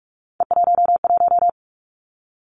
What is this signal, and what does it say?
Morse code, heard as a keyed tone, about 46 wpm, 700 Hz. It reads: E11
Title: Three Challenging Letter-Number Combinations 45wpm